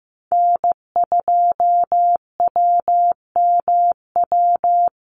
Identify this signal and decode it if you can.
Morse code, heard as a keyed tone, about 15 wpm, 700 Hz. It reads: N2WMW
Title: USA Call Signs Repeated 2x 15wpm